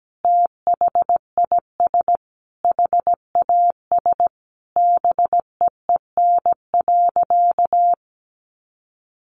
Morse code, heard as a keyed tone, about 17 wpm, 700 Hz. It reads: THIS HAS BEEN.
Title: Sentences from Top 1400 Words 17wpm